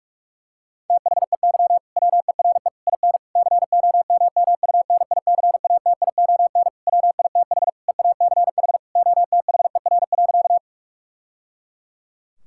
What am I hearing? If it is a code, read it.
Morse code, heard as a keyed tone, about 42 wpm, 700 Hz. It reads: THEY WERE IN COMMUNICATION WITH EACH OTHER.
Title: Sentences from Top 1000 Words - Review Part 6 - 45wpm